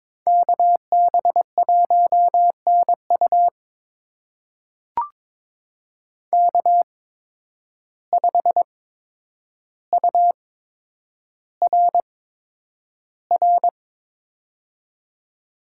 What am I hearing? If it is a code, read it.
Morse code, heard as a keyed tone, about 22 wpm, spaced as for 9 wpm, 700 Hz. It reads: KB1NU K5URR
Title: USA Call Signs - 5 Characters - 8x Character Spacing - 22wpm